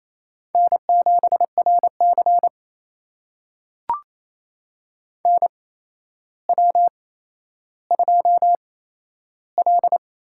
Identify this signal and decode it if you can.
Morse code, heard as a keyed tone, about 28 wpm, spaced as for 12 wpm, 700 Hz. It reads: N7RC NW2L
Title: USA Call Signs - 4 Characters - 8x Character Spacing - 28wpm